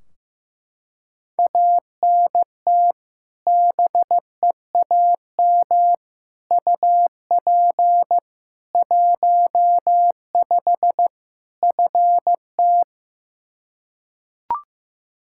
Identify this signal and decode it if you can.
Morse code, heard as a keyed tone, about 16 wpm, 700 Hz. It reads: ANT BEAM UP 15 FT
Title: Practice QSO - Exchange 2 - Info - 15wpm